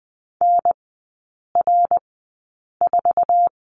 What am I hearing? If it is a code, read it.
Morse code, heard as a keyed tone, about 20 wpm, spaced as for 8 wpm, 700 Hz. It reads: NR4
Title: 053 Lesson 10e - Letter r - Three Characters 20wpm